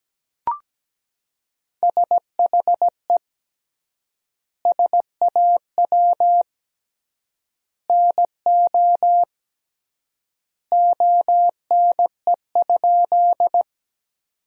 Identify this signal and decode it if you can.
Morse code, heard as a keyed tone, about 17 wpm, 700 Hz. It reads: SHE SAW NO ONE?
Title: 3x Word-Spacing - Sentences from Top 300 Words - 17wpm